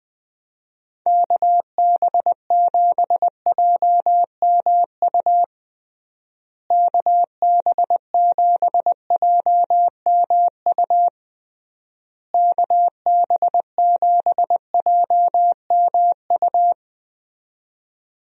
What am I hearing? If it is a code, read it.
Morse code, heard as a keyed tone, about 20 wpm, 700 Hz. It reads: KB7JMU KB7JMU KB7JMU
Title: USA Call Signs - 6 Characters - Repeated 3x - 20wpm